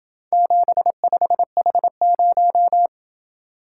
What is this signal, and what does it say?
Morse code, heard as a keyed tone, about 28 wpm, 700 Hz. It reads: 75H0